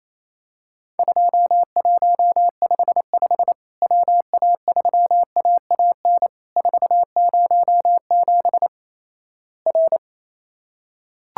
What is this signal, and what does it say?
Morse code, heard as a keyed tone, about 28 wpm, 700 Hz. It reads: 2155 WA3AAN 407 R
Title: Worked All Europe DX Contest - QTC Exchange - 28wpm